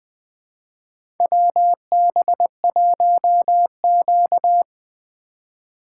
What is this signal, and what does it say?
Morse code, heard as a keyed tone, about 20 wpm, 700 Hz. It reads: WB1Q
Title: USA Call Signs - 3x Character Spacing - 20wpm